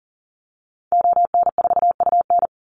Morse code, heard as a keyed tone, about 40 wpm, 700 Hz. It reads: ON4UN
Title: Call Signs - 40wpm